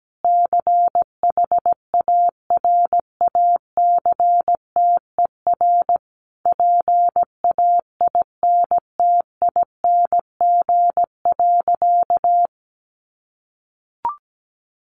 Morse code, heard as a keyed tone, about 17 wpm, 700 Hz. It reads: CHARACTER PAINTING.
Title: Sentences from Top 1200 Words 17wpm